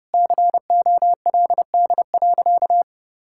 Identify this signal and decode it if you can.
Morse code, heard as a keyed tone, about 30 wpm, 700 Hz. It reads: COLD.
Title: Sentences from Top 1000 Words - Review Part 5 - 30wpm